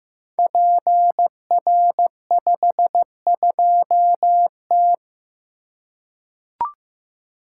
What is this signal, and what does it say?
Morse code, heard as a keyed tone, about 15 wpm, 700 Hz. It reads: PR52T
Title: Code Groups - 5 Letter:Numbers - 15wpm